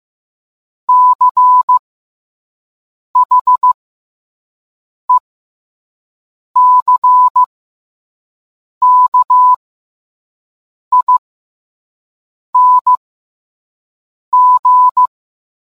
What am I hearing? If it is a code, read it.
Morse code, heard as a keyed tone, about 15 wpm, spaced as for 5 wpm, 1000 Hz. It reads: CHECKING